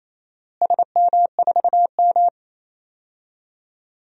Morse code, heard as a keyed tone, about 27 wpm, 700 Hz. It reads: SM4M
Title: DXCC-100 Call Signs - 4 Characters - Repeated 3x - 28wpm